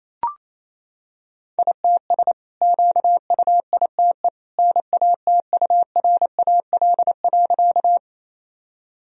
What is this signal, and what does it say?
Morse code, heard as a keyed tone, about 27 wpm, 700 Hz. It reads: ITS QUITE NATURAL.
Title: Sentences from Top 1000 Words - Review Part 4 - 28wpm